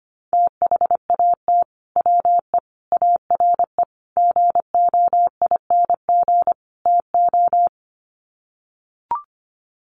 Morse code, heard as a keyed tone, about 25 wpm, 700 Hz. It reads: THAT WE ARE GOING TO
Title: Sets of 5 Words Encore 25wpm